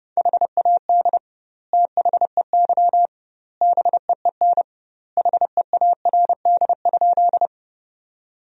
Morse code, heard as a keyed tone, about 30 wpm, 700 Hz. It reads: HAD THEY BEEN HEARD?
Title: 2x Word-Spacing - Sentences from Top 400 Words - 30wpm